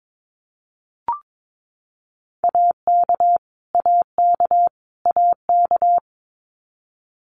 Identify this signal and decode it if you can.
Morse code, heard as a keyed tone, about 22 wpm, 700 Hz. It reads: AK AK AK
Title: US States Repeated 3x 22wpm